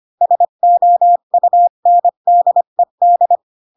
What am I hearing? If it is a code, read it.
Morse code, heard as a keyed tone, about 25 wpm, 700 Hz. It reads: SOUNDED
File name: Pinocchio / Pinocchio ch2 20 25 ik1hge